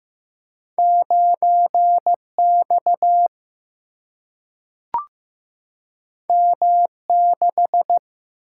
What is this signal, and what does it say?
Morse code, heard as a keyed tone, about 15 wpm, 700 Hz. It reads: 9X M6
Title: Single-Letter Single-Number 15wpm